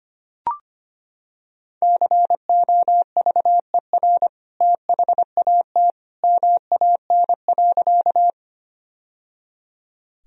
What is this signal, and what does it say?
Morse code, heard as a keyed tone, about 25 wpm, 700 Hz. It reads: COVER THAT MAN.
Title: Sentences from Top 1000 Words - Review Part 1 - 25wpm